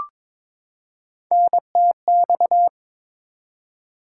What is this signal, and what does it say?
Morse code, heard as a keyed tone, about 22 wpm, 700 Hz. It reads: NTX
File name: practice / 22wpm / ARRL Sections v2 - 22wpm